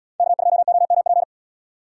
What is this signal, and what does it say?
Morse code, heard as a keyed tone, about 78 wpm, 700 Hz. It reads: K2OMP